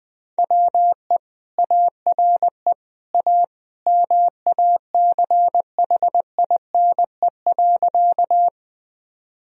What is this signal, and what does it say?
Morse code, heard as a keyed tone, about 20 wpm, 700 Hz. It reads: WE ARE A MACHINE.
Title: Sentences from Top 1000 Words - Review Part 2 - 20wpm